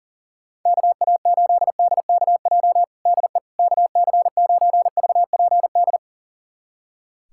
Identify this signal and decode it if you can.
Morse code, heard as a keyed tone, about 40 wpm, 700 Hz. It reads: KA8DKJ DE KC9VPD
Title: USA Callsign DE Callsign - 6 Characters - 40wpm